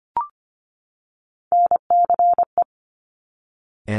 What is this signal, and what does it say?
Morse code, heard as a keyed tone, about 25 wpm, 700 Hz. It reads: NCE
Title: Top Three Letter Combinations in Words 25wpm